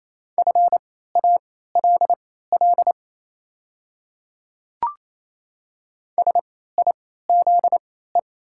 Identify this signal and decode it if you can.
Morse code, heard as a keyed tone, about 28 wpm, spaced as for 16 wpm, 690 Hz. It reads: FALL SIZE
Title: 3x Character Spacing - Top 500 Words - 28wpm